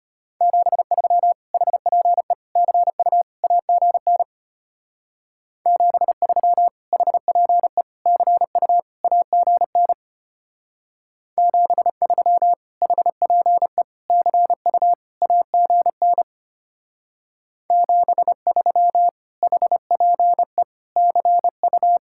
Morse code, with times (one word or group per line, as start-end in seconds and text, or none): 0.4-1.3 73
1.5-2.3 HPE
2.6-3.2 CU
3.4-4.2 AGN
5.7-6.7 73
6.9-7.8 HPE
8.1-8.8 CU
9.0-9.9 AGN
11.4-12.5 73
12.8-13.8 HPE
14.1-14.9 CU
15.2-16.2 AGN
17.7-19.1 73
19.4-20.6 HPE
21.0-22.0 CU